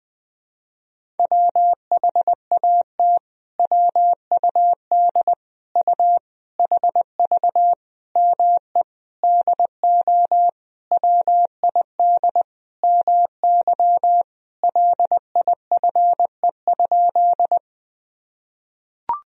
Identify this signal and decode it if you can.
Morse code, heard as a keyed tone, about 21 wpm, 700 Hz. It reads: WHAT WUD U HV ME DO WID MY LIFE?
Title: Sentences from Top 300 Words and Abbreviations 20wpm